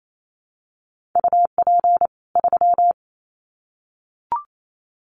Morse code, heard as a keyed tone, about 28 wpm, 700 Hz. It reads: UP 3
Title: Common QSO Phrases 28wpm